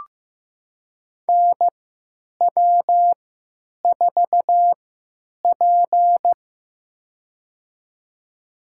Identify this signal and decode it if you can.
Morse code, heard as a keyed tone, about 15 wpm, spaced as for 9 wpm, 700 Hz. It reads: NW4P
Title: USA Call Signs - 4 Characters - 3x Character Spacing - 15wpm